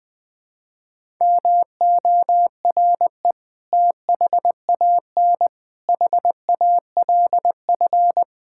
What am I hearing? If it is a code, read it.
Morse code, heard as a keyed tone, about 20 wpm, 700 Hz. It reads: MORE THAN HALF
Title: Build A Phrase - 5-Word Phrases - 20wpm